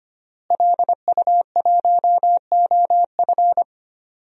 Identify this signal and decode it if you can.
Morse code, heard as a keyed tone, about 25 wpm, 700 Hz. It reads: LU1OF
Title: DXCC-100 Call Signs - 5 Characters - Repeated 3x - 25wpm